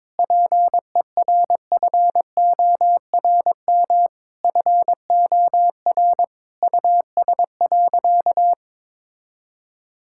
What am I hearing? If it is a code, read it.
Morse code, heard as a keyed tone, about 22 wpm, 700 Hz. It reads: PERFORM FOR US.
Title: Sentences from Top 1000 Words - Review Part 5 - 22wpm